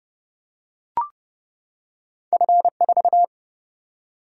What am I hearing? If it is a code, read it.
Morse code, heard as a keyed tone, about 30 wpm, 700 Hz. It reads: F4
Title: Single-Letter Single-Number 30wpm